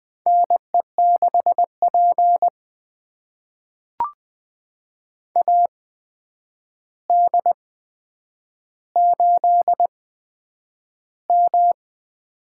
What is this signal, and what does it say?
Morse code, heard as a keyed tone, about 20 wpm, spaced as for 9 wpm, 700 Hz. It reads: NE6P AD8M
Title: USA Call Signs - 4 Characters - 8x Character Spacing - 20wpm